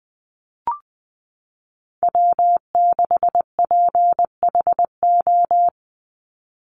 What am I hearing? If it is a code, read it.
Morse code, heard as a keyed tone, about 20 wpm, 700 Hz. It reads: W6PHO
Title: Call Signs Repeated 2x - 20wpm